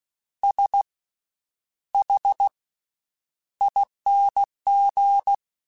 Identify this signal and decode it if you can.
Morse code, heard as a keyed tone, about 16 wpm, 800 Hz. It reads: S H ING
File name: Special INGs-mp3s / Special INGs-mp3s / Slow ing @15